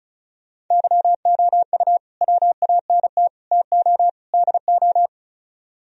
Morse code, heard as a keyed tone, about 44 wpm, 700 Hz. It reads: YOU WANT TO DO